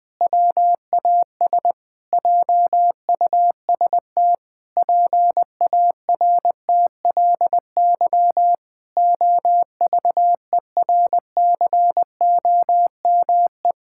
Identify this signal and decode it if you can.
Morse code, heard as a keyed tone, about 21 wpm, 700 Hz. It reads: WAS JUST PARTLY OVERCOME
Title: Pangram Challenge v2 - 20wpm